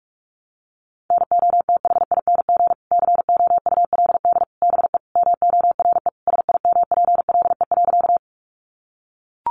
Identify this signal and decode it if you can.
Morse code, heard as a keyed tone, about 44 wpm, 700 Hz. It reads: NOTHING COULD BE MORE SIMPLE.
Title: Sentences from Top 500 Words 45wpm